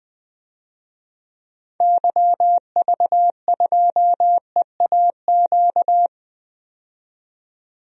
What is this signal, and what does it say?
Morse code, heard as a keyed tone, about 20 wpm, 700 Hz. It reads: YV2EAQ
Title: DXCC-100 Call Signs - 6 Characters - 20wpm